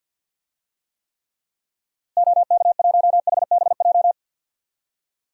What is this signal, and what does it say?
Morse code, heard as a keyed tone, about 53 wpm, 700 Hz. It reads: OK1HBJ
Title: DXCC-100 Call Signs - 6 Characters - Repeated 3x - 50wpm